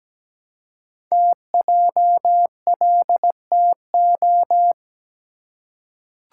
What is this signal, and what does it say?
Morse code, heard as a keyed tone, about 17 wpm, 700 Hz. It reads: TJLTO
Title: Code Groups - 5 Letters - 17wpm